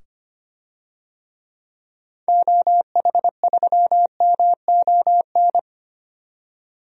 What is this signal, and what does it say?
Morse code, heard as a keyed tone, about 25 wpm, 700 Hz. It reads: OH3MON
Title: DXCC-100 Call Signs - 6 Characters - 25wpm